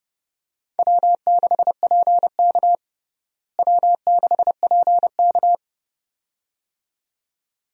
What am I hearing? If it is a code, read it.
Morse code, heard as a keyed tone, about 30 wpm, 700 Hz. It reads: W6PK W6PK
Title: USA Call Signs - 4 Characters - Repeated 2x - 30wpm